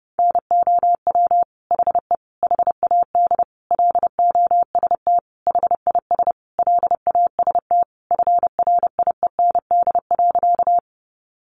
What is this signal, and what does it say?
Morse code, heard as a keyed tone, about 30 wpm, 700 Hz. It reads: NOW HE HAD LOST HIS LAST FRIEND.
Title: Sentences from Top 1400 Words 30wpm